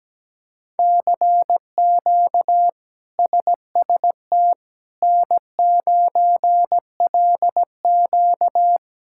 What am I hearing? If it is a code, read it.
Morse code, heard as a keyed tone, about 17 wpm, 700 Hz. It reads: CQ SST N9LQ
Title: K1USN SST QSO Practice - 17wpm